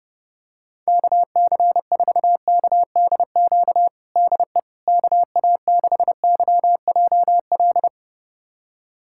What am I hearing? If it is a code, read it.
Morse code, heard as a keyed tone, about 30 wpm, 700 Hz. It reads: KC4KDQ DE KA6YJL
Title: USA Callsign DE Callsign - 6 Characters - 30wpm